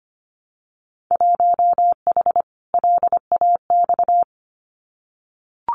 Morse code, heard as a keyed tone, about 25 wpm, 700 Hz. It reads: 1H LAX
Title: Winter Field Day Exchanges 25wpm